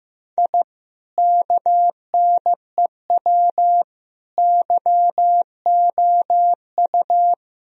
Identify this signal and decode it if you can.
Morse code, heard as a keyed tone, about 16 wpm, 700 Hz. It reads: I KNEW YOU
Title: Sentences from Top 1000 Words - Review Part 2 - 15wpm